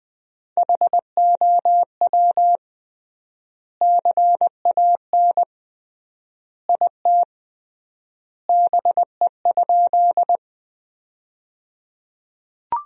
Morse code, heard as a keyed tone, about 20 wpm, 700 Hz. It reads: HOW CAN IT BE?
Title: 3x Word-Spacing - Sentences from Top 100 Words - 20wpm